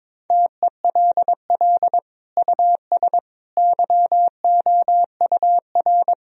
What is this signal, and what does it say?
Morse code, heard as a keyed tone, about 22 wpm, 700 Hz. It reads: TELL US YOUR
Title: Sentences from Top 1000 Words - Review Part 4 - 22wpm